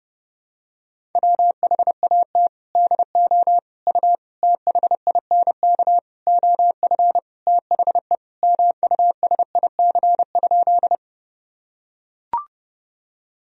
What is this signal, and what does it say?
Morse code, heard as a keyed tone, about 32 wpm, 700 Hz. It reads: WHAT DO U THINK OF THE MUSIC?
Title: Sentences from Top 300 Words and Abbreviations 30wpm